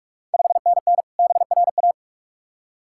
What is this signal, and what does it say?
Morse code, heard as a keyed tone, about 44 wpm, 690 Hz. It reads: 5NN BRA
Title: Ontario QSO Party Exchange 45wpm